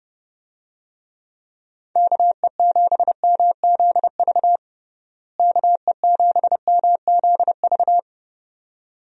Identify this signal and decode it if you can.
Morse code, heard as a keyed tone, about 30 wpm, 700 Hz. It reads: KE7MZV KE7MZV
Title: USA Call Signs - 6 Characters - Repeated 2x - 30wpm